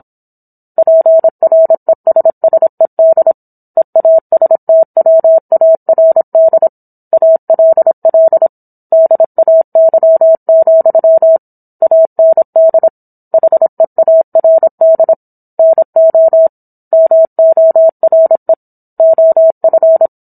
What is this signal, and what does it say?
Morse code, heard as a keyed tone, about 27 wpm, 650 Hz. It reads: PRESSED EASTWARD ALL DAY, AND HEARD NO MORE OF